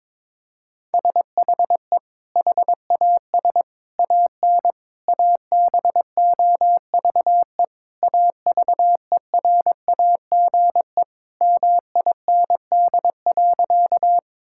Morse code, heard as a keyed tone, about 22 wpm, 700 Hz. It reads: SHE HAS AN ABOVE AVERAGE MIND.
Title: Sentences from Top 1000 Words - Review Part 6 - 22wpm